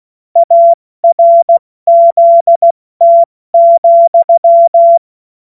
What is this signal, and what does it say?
Morse code, heard as a keyed tone, about 16 wpm, 680 Hz. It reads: ARZT,
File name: Morsekurs / Witze / Witz 13. Dezember 2022 70 F80